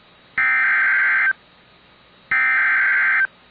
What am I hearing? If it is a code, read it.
Emergency Alert System (SAME) tones (header bursts), heard as harsh buzzing bursts. It is heard from 0.4 to 3.2 s.